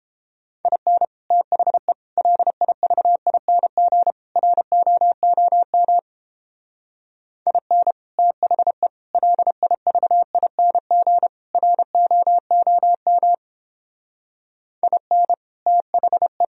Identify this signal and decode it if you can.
Morse code, heard as a keyed tone, about 29 wpm, 700 Hz. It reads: IN THE LIVING ROOM IN THE LIVING ROOM IN THE